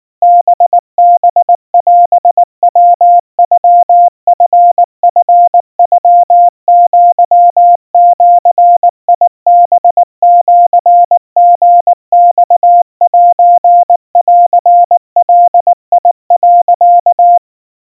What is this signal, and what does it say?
Morse code (Japanese Wabun code), heard as a keyed tone, about 19 wpm, 700 Hz. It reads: ハハオヤノチチノアジハシリマセンガ、
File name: kawa / Morse sample / kamyu 70 cpm 700 Hz Crispy